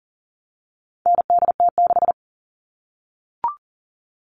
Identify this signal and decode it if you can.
Morse code, heard as a keyed tone, about 40 wpm, 700 Hz. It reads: NDT6
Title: All Cut Numbers - 4-Digit 40wpm